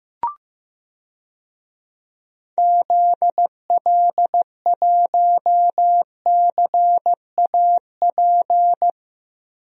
Morse code, heard as a keyed tone, about 15 wpm, 700 Hz. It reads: ZL1CAP
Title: DXCC-100 Call Signs - 6 Characters - Repeated 2x - 15wpm